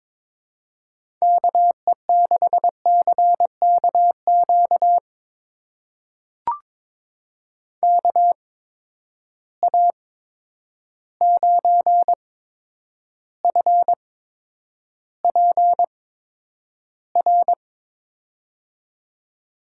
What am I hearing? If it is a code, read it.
Morse code, heard as a keyed tone, about 22 wpm, spaced as for 9 wpm, 700 Hz. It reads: KE6CKQ KA9FPR
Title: USA Call Signs - 6 Characters - 8x Character Spacing - 22wpm